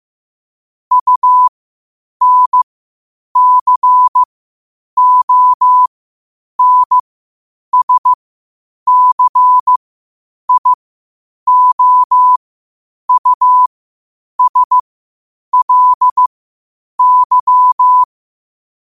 Morse code, heard as a keyed tone, about 15 wpm, spaced as for 8 wpm, 1000 Hz. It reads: UNCONSCIOUSLY
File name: qotdfiles / 8wpm